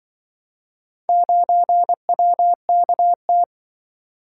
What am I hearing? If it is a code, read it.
Morse code, heard as a keyed tone, about 25 wpm, 700 Hz. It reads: 9WKT
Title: Speed-Racing - Four Letter-Number Combinations - 20wpm